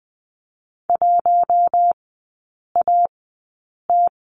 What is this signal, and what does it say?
Morse code, heard as a keyed tone, about 20 wpm, spaced as for 8 wpm, 700 Hz. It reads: 1AT
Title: 133 Lesson 22e - Question Mark - Three Characters 20wpm